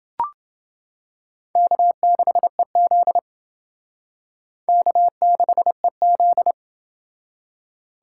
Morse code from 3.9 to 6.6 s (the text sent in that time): K6EZ